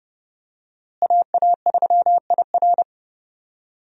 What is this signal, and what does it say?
Morse code, heard as a keyed tone, about 30 wpm, 700 Hz. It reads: AA3IR
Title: USA Call Signs - 5 Characters - Repeated 3x - 30wpm